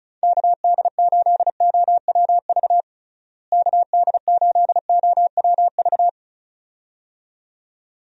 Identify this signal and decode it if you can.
Morse code, heard as a keyed tone, about 36 wpm, 700 Hz. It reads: KD8OWV KD8OWV
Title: USA Call Signs - 6 Characters - Repeated 2x - 35wpm